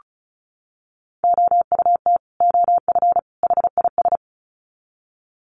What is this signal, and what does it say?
Morse code, heard as a keyed tone, about 34 wpm, 700 Hz. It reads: OUT OF HIS
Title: Sets of 3 Words 35wpm